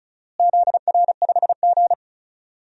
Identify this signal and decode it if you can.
Morse code, heard as a keyed tone, about 36 wpm, 700 Hz. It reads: ZR5G
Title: DXCC-100 Call Signs - 4 Characters - Repeated 2x - 35wpm